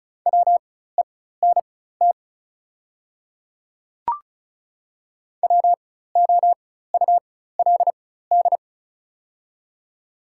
Morse code, heard as a keyed tone, about 34 wpm, spaced as for 16 wpm, 690 Hz. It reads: WENT WOULD
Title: 4x Character Spacing - Top 500 Words - 35wpm